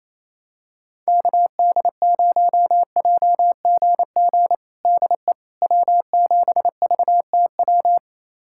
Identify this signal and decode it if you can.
Morse code, heard as a keyed tone, about 28 wpm, 700 Hz. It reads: KD0JGG DE W7VTW
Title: USA Callsign DE Callsign 28wpm